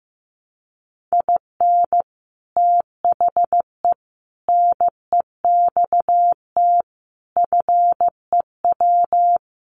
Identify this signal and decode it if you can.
Morse code, heard as a keyed tone, about 15 wpm, 700 Hz. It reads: IN THE NEXT FEW
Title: Sets of 5 Words Encore 15wpm